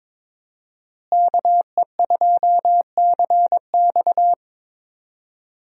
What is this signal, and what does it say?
Morse code, heard as a keyed tone, about 22 wpm, 700 Hz. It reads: KE2CX
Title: USA Call Signs - 5 Characters - Repeated 2x - 22wpm